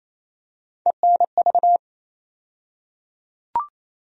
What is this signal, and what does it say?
Morse code, heard as a keyed tone, about 28 wpm, 700 Hz. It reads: ENV
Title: USA Call Sign Postfixes - 3 Letters 28wpm